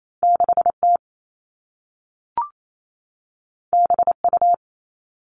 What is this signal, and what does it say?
Morse code, heard as a keyed tone, about 28 wpm, 700 Hz. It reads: 6T BU
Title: All Cut Numbers - 2-Digit 28wpm